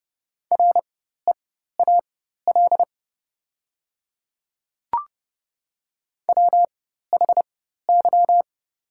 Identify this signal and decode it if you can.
Morse code, heard as a keyed tone, about 30 wpm, spaced as for 14 wpm, 690 Hz. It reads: REAL WHY
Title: 4x Character Spacing - Top 300 Words - 30wpm